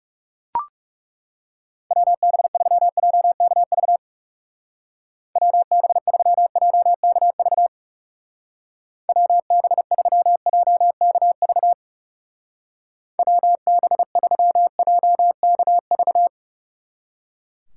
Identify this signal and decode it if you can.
Morse code, heard as a keyed tone, about 39 wpm, 700 Hz. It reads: WB3JKV WB3JKV WB3JKV WB3JKV